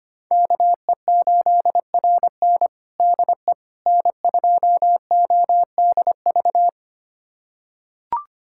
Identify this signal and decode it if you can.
Morse code, heard as a keyed tone, about 25 wpm, 700 Hz. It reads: KE8RN DE N2ODV
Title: USA Callsign DE Callsign - 5 Characters - 25wpm